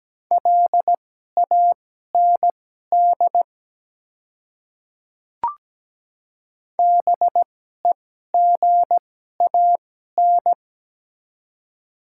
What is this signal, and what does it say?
Morse code, heard as a keyed tone, about 17 wpm, spaced as for 12 wpm, 700 Hz. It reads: LAND BEGAN
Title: 2x Character Spacing - Top 300 Words - 17wpm